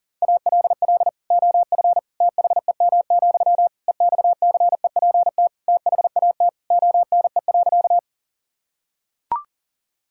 Morse code, heard as a keyed tone, about 37 wpm, 700 Hz. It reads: ALL OF THEM, EXCEPT THAT ONE.
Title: Sentences from Top 1600 Words 40wpm